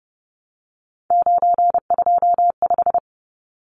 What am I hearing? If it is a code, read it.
Morse code, heard as a keyed tone, about 30 wpm, 700 Hz. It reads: 925
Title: Numbers - 3-Digit 30wpm